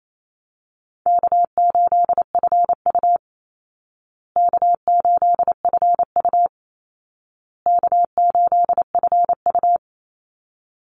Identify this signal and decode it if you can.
Morse code, heard as a keyed tone, about 28 wpm, 700 Hz. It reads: K8FU K8FU K8FU
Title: Easy Call Signs Repeated 3x - 28wpm